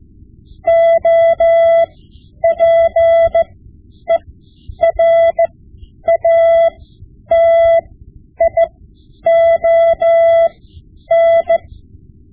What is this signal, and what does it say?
Morse code, heard as a keyed tone, about 15 wpm, spaced as for 9 wpm, 670 Hz. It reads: OPERATION
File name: Morse Lessons / Morse Lessons / Tape-5B